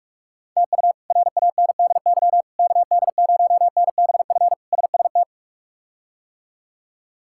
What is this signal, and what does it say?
Morse code, heard as a keyed tone, about 46 wpm, 700 Hz. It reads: TU RANDY KD0NBF SST
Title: K1USN SST QSO Practice - 45wpm